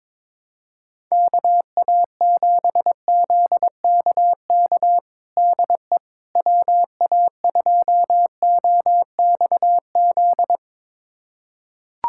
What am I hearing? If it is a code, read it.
Morse code, heard as a keyed tone, about 22 wpm, 700 Hz. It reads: KA7ZKK DE WA2OXZ
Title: USA Callsign DE Callsign - 6 Characters - 22wpm